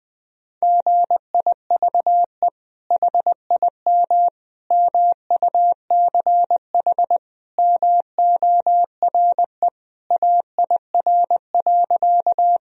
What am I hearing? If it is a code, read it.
Morse code, heard as a keyed tone, about 20 wpm, 700 Hz. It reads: GIVE HIM MUCH MORE AIR.
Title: Sentences from Top 1000 Words - Review Part 1 - 20wpm